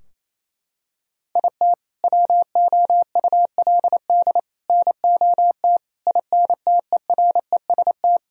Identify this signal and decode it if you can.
Morse code, heard as a keyed tone, about 28 wpm, 700 Hz. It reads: IT WOULD NOT INTEREST
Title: Sentences from Top 1000 Words - Review Part 2 - 28wpm